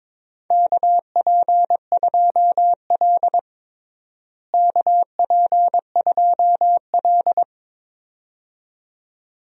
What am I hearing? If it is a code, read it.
Morse code, heard as a keyed tone, about 22 wpm, 700 Hz. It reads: KP2L KP2L
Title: DXCC-100 Call Signs - 4 Characters - Repeated 2x - 22wpm